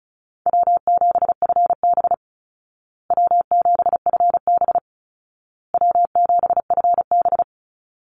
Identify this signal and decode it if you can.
Morse code, heard as a keyed tone, about 34 wpm, 700 Hz. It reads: W7FB W7FB W7FB
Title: Easy Call Signs Repeated 3x - 35wpm